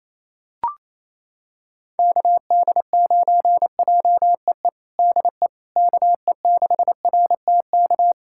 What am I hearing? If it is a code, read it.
Morse code, heard as a keyed tone, about 28 wpm, 700 Hz. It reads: KD9JEE DE KE6RTK
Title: USA Callsign DE Callsign - 6 Characters - 28wpm